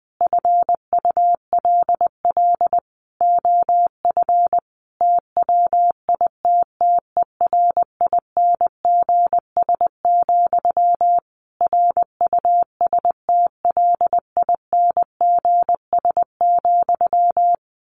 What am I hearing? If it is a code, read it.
Morse code, heard as a keyed tone, about 20 wpm, 700 Hz. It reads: FULL OF TWITTERINGS, RUSTLINGS,
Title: A Sound Of Thunder 20wpm